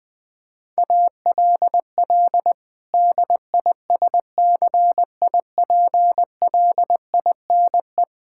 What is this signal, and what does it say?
Morse code, heard as a keyed tone, about 20 wpm, 700 Hz. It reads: ALL DISCIPLINE
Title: Sentences from Top 1000 Words - Review Part 6 - 20wpm